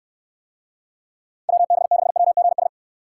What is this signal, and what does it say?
Morse code, heard as a keyed tone, about 69 wpm, 700 Hz. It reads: WB6WGS